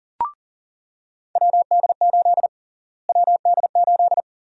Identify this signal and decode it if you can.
Morse code, heard as a keyed tone, about 37 wpm, 700 Hz. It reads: WD8 WD8
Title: USA Call Sign Prefixes - Repeated 2x - 40wpm